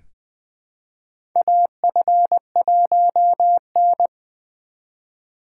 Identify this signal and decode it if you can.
Morse code, heard as a keyed tone, about 20 wpm, 700 Hz. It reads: AF1N
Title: USA Call Signs - 4 Characters - 4x Character Spacing - 20wpm